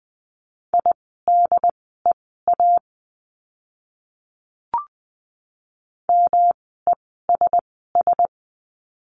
Morse code, heard as a keyed tone, about 20 wpm, spaced as for 15 wpm, 700 Hz. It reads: IDEA MESS
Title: 108 Lesson 18f - Letter m - Words 20wpm